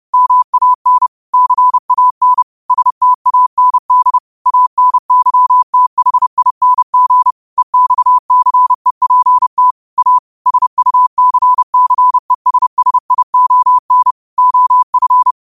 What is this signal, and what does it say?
Morse code, heard as a keyed tone, about 30 wpm, 1000 Hz. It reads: MAN CAN STAND ANYTHING EXCEPT A SUCCESSION OF
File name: qotdfiles / 30wpm